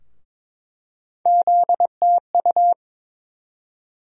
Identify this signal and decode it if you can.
Morse code, heard as a keyed tone, about 22 wpm, 700 Hz. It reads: ZTU
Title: USA Call Sign Postfixes - 3 Letters 22wpm